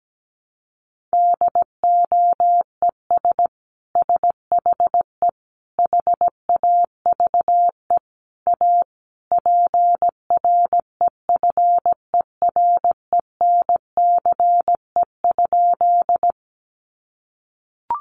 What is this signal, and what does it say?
Morse code, heard as a keyed tone, about 17 wpm, 700 Hz. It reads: DOES SHE HAVE A PREFERENCE?
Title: Sentences from Top 1500 Words 17wpm